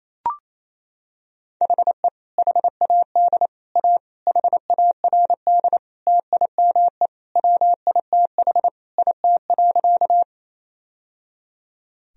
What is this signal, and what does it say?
Morse code, heard as a keyed tone, about 28 wpm, 700 Hz. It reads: HE HAD A HARD TIME WITH IT.
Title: Sentences from Top 1000 Words - Review Part 1 - 28wpm